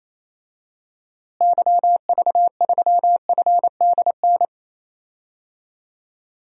Morse code, heard as a keyed tone, about 28 wpm, 700 Hz. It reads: YV3FDN
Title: DXCC-100 Call Signs - 6 Characters - Repeated 3x - 28wpm